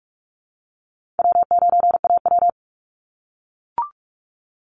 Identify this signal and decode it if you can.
Morse code, heard as a keyed tone, about 44 wpm, 700 Hz. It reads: W9AW
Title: Easy Call Signs - 45wpm